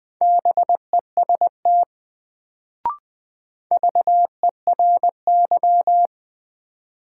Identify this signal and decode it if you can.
Morse code, heard as a keyed tone, about 20 wpm, 700 Hz. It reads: BEST VERY
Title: Sets of 2 Words - New Word Drill - 20wpm